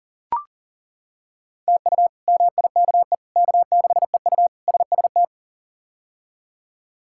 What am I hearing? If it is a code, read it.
Morse code, heard as a keyed tone, about 40 wpm, 700 Hz. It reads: TU MIKE K6EU SST
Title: K1USN SST QSO Practice - 40wpm